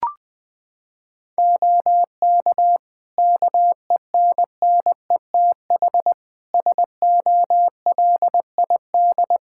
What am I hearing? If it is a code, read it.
Morse code, heard as a keyed tone, about 21 wpm, 700 Hz. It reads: OK KENNETH SOLID
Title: Practice QSO - Exchange 3 - More Info - 20wpm